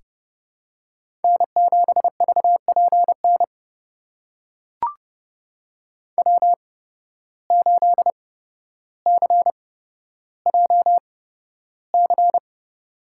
Morse code, heard as a keyed tone, about 29 wpm, spaced as for 13 wpm, 700 Hz. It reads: N7VPN W8CJC
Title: USA Call Signs - 5 Characters - 8x Character Spacing - 30wpm